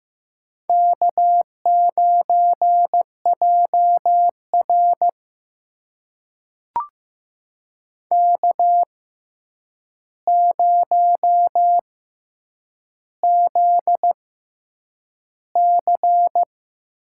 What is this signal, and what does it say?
Morse code, heard as a keyed tone, about 15 wpm, spaced as for 8 wpm, 700 Hz. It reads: K9JR K0ZC
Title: USA Call Signs - 4 Characters - 6x Character Spacing - 15wpm